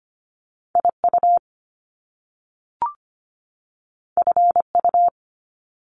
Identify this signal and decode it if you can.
Morse code, heard as a keyed tone, about 25 wpm, 700 Hz. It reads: I U F U
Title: Top Two Letter Combinations in Words 25wpm